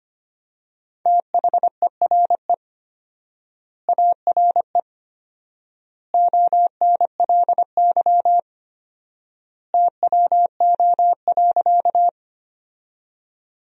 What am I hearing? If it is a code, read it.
Morse code, heard as a keyed tone, about 25 wpm, 700 Hz. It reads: THERE ARE ONLY TWO.
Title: 4x Word-Spacing - Sentences from Top 200 Words - 25wpm